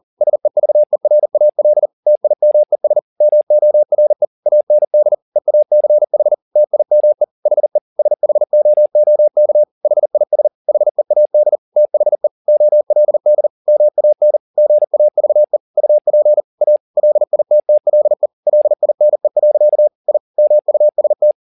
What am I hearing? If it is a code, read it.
Morse code, heard as a keyed tone, about 40 wpm, 600 Hz. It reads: SEVERAL TIMES MORE AND EACH TIME HE SHOOK HIS HEAD THE OLD MAN GAVE UP A LITTLE LINE. I MUST
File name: the old man and the sea 2 40wpm